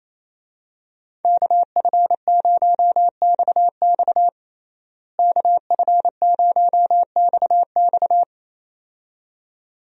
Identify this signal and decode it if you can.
Morse code, heard as a keyed tone, about 28 wpm, 700 Hz. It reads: KF0XX KF0XX
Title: USA Call Signs - 5 Characters - Repeated 2x - 28wpm